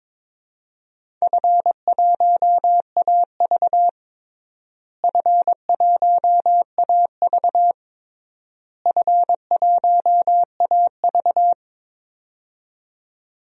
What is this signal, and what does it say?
Morse code, heard as a keyed tone, about 22 wpm, 700 Hz. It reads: F1AV F1AV F1AV
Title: DXCC-100 Call Signs - 4 Characters - Repeated 3x - 22wpm